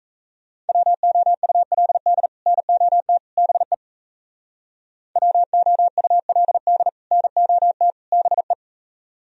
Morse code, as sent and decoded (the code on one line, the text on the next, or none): .-- --- ..- .-.. -.. / -. --- - / -... / .-- --- ..- .-.. -.. / -. --- - / -... .
WOULD NOT BE WOULD NOT BE